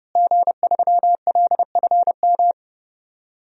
Morse code, heard as a keyed tone, about 32 wpm, 700 Hz. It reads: G3LFM
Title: DXCC-100 Call Signs - 5 Characters - Repeated 3x - 30wpm